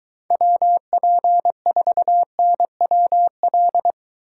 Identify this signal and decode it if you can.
Morse code, heard as a keyed tone, about 23 wpm, 700 Hz. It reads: WP4NWL